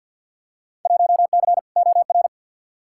Morse code, heard as a keyed tone, about 49 wpm, 700 Hz. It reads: 1C OR
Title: Field Day Exchanges - 50wpm